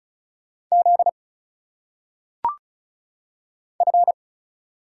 Morse code, heard as a keyed tone, about 35 wpm, 700 Hz. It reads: Z F
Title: Single Letters - Phonetic Pronunciation 35wpm